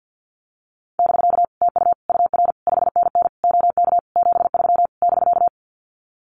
Morse code, heard as a keyed tone, about 50 wpm, 700 Hz. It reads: <BK> TU UR 5NN OK 73 <BK>
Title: Common QSO Phrases Encore 50wpm